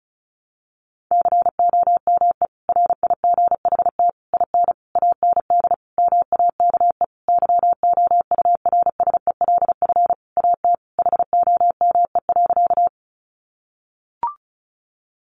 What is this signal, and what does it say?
Morse code, heard as a keyed tone, about 34 wpm, 700 Hz. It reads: COME RIGHT IN AND MAKE YOURSELF AT HOME.
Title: Sentences from Top 600 Words 35wpm